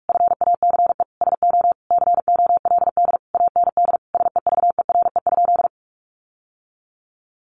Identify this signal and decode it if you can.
Morse code, heard as a keyed tone, about 44 wpm, 700 Hz. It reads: FACE SO COLD AND SEVERE?
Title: Sentences from Top 1400 Words 45wpm